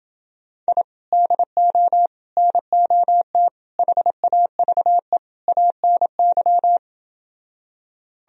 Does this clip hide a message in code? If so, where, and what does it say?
Yes: Morse code from 0.7 to 6.8 s, heard as a keyed tone, about 31 wpm, 700 Hz. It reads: I DO NOT HAVE ANY